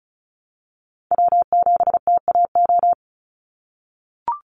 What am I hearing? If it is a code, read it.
Morse code, heard as a keyed tone, about 36 wpm, 700 Hz. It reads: W7TAO
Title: NA SOTA Chaser Callsigns 35wpm